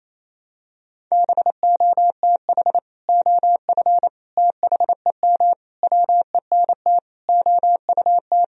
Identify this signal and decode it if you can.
Morse code, heard as a keyed tone, about 28 wpm, 700 Hz. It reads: BOTH OF THEM WENT OUT
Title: Sentences from Top 1000 Words - Review Part 2 - 28wpm